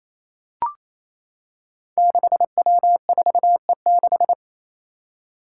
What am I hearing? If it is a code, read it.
Morse code, heard as a keyed tone, about 28 wpm, 700 Hz. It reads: 6W4E6
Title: Five Challenging Letter-Number Combinations 28wpm